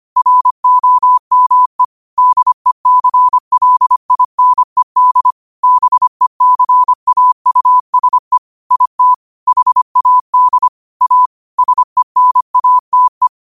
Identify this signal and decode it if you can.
Morse code, heard as a keyed tone, about 25 wpm, 1000 Hz. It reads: ROME DECLINED BECAUSE IT HAD A SENATE